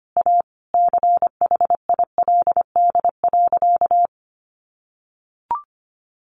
Morse code, heard as a keyed tone, about 25 wpm, 700 Hz. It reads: A CHILD.
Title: Sentences from Top 1400 Words 25wpm